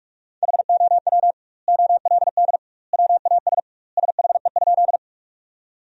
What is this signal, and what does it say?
Morse code, heard as a keyed tone, about 41 wpm, 700 Hz. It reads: HOW OLD WAS SHE?
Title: 2x Word-Spacing - Sentences from Top 200 Words - 45wpm